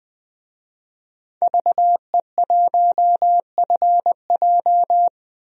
Morse code, heard as a keyed tone, about 20 wpm, 700 Hz. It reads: VE1FJ
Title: DXCC-100 Call Signs - 5 Characters - 20wpm